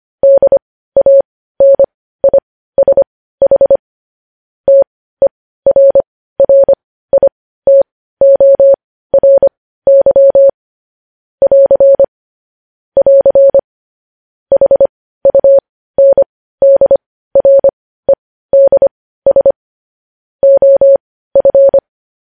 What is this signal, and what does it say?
Morse code, heard as a keyed tone, about 25 wpm, spaced as for 15 wpm, 550 Hz. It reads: DANISH TERRITORY <AR> <AR> HUNDREDS OF
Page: News Headlines in Morse Code at 15 WPM